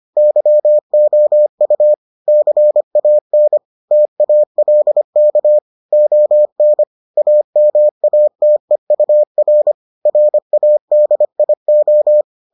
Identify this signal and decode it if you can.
Morse code, heard as a keyed tone, about 25 wpm, 600 Hz. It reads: YOU CAN TALK ON AMATEUR RADIO